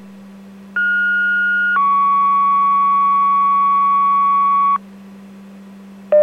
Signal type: two-tone page